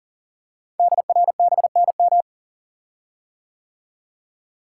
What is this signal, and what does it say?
Morse code, heard as a keyed tone, about 42 wpm, 690 Hz. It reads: DRBNM
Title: Code Groups - 5 Letters - 40wpm